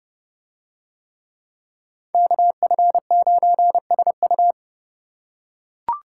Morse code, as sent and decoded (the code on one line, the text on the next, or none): -.- ..-. ----. ... ..-
KF9SU